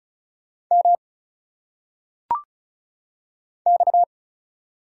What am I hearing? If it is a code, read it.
Morse code, heard as a keyed tone, about 34 wpm, 700 Hz. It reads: M X
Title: Single Letters - Phonetic Pronunciation 35wpm